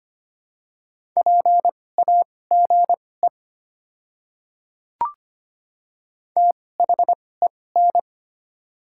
Morse code, heard as a keyed tone, about 25 wpm, spaced as for 18 wpm, 700 Hz. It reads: PAGE THEN
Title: 2x Character Spacing - Top 300 Words - 25wpm